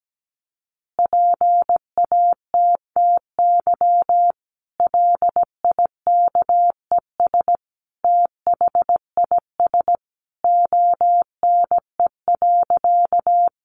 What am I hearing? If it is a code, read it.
Morse code, heard as a keyed tone, about 17 wpm, 700 Hz. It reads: PATTY LIKES THIS ONE.
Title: Additional Easy Phrases 17wpm